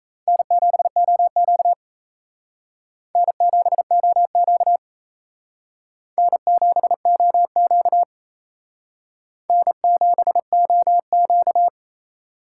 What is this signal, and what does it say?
Morse code, heard as a keyed tone, about 34 wpm, 700 Hz. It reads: N7OQ N7OQ N7OQ N7OQ
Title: Speed Racing - USA 1x2 Call Signs 28wpm